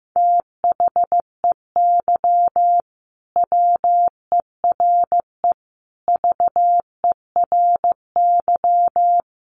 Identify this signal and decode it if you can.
Morse code, heard as a keyed tone, about 14 wpm, 700 Hz. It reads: THEY WERE VERY
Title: Sentences from Top 1100 Words 15wpm